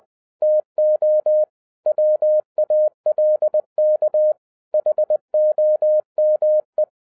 Morse code, heard as a keyed tone, about 20 wpm, 610 Hz. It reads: TO WALK HOME
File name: 2011-07-mp3 / n33 THE 3.11 2 20wpm